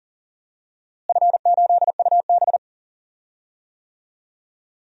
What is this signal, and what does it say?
Morse code, heard as a keyed tone, about 40 wpm, 700 Hz. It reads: F8UB
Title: DXCC-100 Call Signs - 4 Characters - Repeated 2x - 40wpm